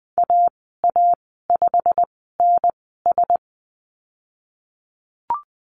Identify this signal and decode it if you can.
Morse code, heard as a keyed tone, about 20 wpm, spaced as for 15 wpm, 700 Hz. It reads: AA5NS
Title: 154 Lesson 25g - Letter p - Call Signs 20wpm